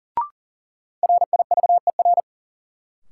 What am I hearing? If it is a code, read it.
Morse code, heard as a keyed tone, about 37 wpm, 700 Hz. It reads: RIVER
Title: Sentences from Top 300 Words - New Word Drill - 40wpm